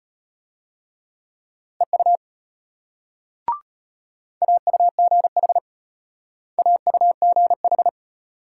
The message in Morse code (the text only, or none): EU AUGH AUGH